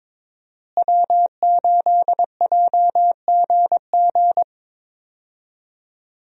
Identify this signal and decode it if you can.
Morse code, heard as a keyed tone, about 21 wpm, 700 Hz. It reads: W8JGG
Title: USA Call Signs - 5 Characters - Repeated 3x - 22wpm